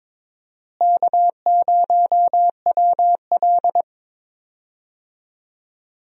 Morse code, heard as a keyed tone, about 21 wpm, 700 Hz. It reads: K0WL
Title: USA Call Signs - 4 Characters - Repeated 3x - 22wpm